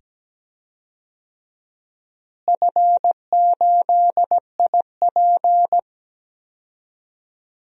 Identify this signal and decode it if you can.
Morse code, heard as a keyed tone, about 17 wpm, 700 Hz. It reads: F8IP
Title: DXCC-100 Call Signs - 4 Characters - Repeated 2x - 17wpm